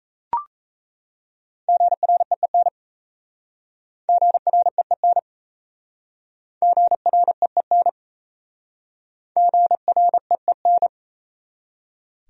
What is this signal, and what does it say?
Morse code, heard as a keyed tone, about 37 wpm, 690 Hz. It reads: GREEN GREEN GREEN GREEN